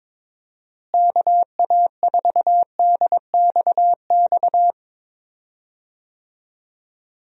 Morse code, heard as a keyed tone, about 22 wpm, 700 Hz. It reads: KA4DXX
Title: USA Call Signs - 6 Characters - Repeated 3x - 22wpm